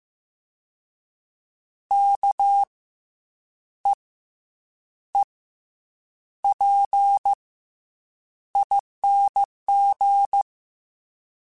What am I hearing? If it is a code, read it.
Morse code, heard as a keyed tone, about 15 wpm, spaced as for 7 wpm, 800 Hz. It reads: KEEPING
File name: Special INGs-mp3s / Special INGs-mp3s / Fast ing @13